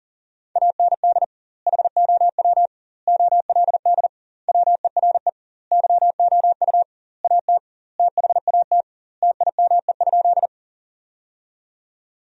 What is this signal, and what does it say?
Morse code, heard as a keyed tone, about 40 wpm, 700 Hz. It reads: AND HOW OLD WERE YOU AT THAT TIME?
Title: 2x Word-Spacing - Sentences from Top 200 Words - 40wpm